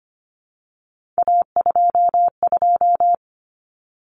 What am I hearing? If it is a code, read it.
Morse code, heard as a keyed tone, about 25 wpm, 700 Hz. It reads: A22
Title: Common Cut Numbers - 3-Digit 25wpm